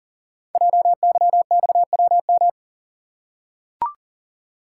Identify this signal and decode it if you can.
Morse code, heard as a keyed tone, about 39 wpm, 690 Hz. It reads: JYXWM
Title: Code Groups - 5 Letters - 40wpm